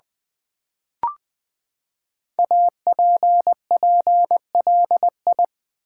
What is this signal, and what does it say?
Morse code, heard as a keyed tone, about 20 wpm, 700 Hz. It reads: APPLI
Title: Build A Word - 5 Letters At A Time - 10-Letter Words - 20wpm